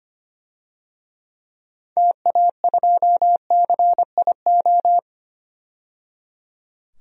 Morse code, heard as a keyed tone, about 25 wpm, 700 Hz. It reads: TA2CIO
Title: DXCC-100 Call Signs - 6 Characters - 25wpm